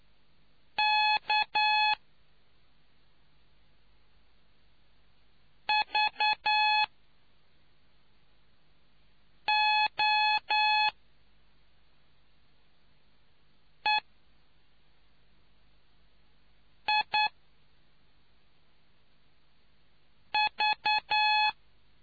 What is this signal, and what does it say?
Morse code, heard as a keyed tone, about 9 wpm, 850 Hz. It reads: K V O E I V